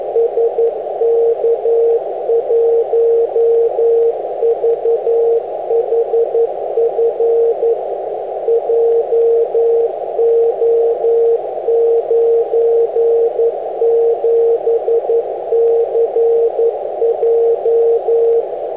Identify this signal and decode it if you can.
Morse code, heard as a keyed tone, about 11 wpm, 460 Hz, with noise. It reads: SK1VHF JO97CJ